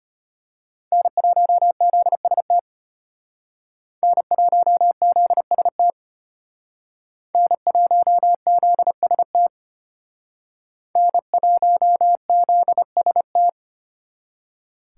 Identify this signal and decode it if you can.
Morse code, heard as a keyed tone, about 33 wpm, 700 Hz. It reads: N1ZST N1ZST N1ZST N1ZST